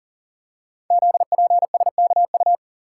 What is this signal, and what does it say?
Morse code, heard as a keyed tone, about 40 wpm, 700 Hz. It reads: ZPSKU
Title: Code Groups - 5 Letter:Numbers - 40wpm